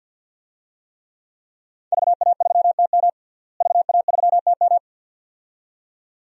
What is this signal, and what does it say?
Morse code, heard as a keyed tone, about 47 wpm, 700 Hz. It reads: VA3TM VA3TM
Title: DXCC-100 Call Signs - 5 Characters - Repeated 2x - 50wpm